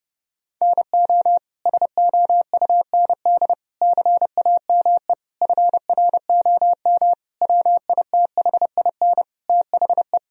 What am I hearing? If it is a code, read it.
Morse code, heard as a keyed tone, about 30 wpm, 700 Hz. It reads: NO SOUND CAME FROM WITHIN THE
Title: Sentences from Top 1000 Words - Review Part 4 - 30wpm